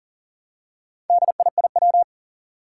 Morse code, heard as a keyed tone, about 43 wpm, 700 Hz. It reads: DIIW
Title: Four Challenging Letter-Number Combinations 40wpm